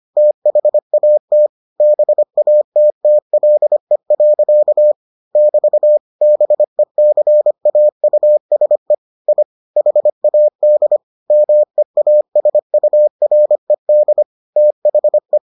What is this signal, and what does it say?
Morse code, heard as a keyed tone, about 25 wpm, 600 Hz. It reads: THAT BATTLE. <BT> BECAUSE I HAD MEASURED THE